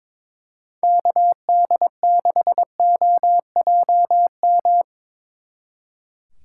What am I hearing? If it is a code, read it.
Morse code, heard as a keyed tone, about 24 wpm, 700 Hz. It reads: KD6OJM
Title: Speed Racing - USA Call Signs 22wpm